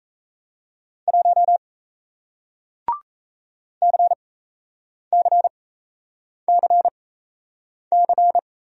Morse code, heard as a keyed tone, about 38 wpm, spaced as for 8 wpm, 700 Hz. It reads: 1 CCCC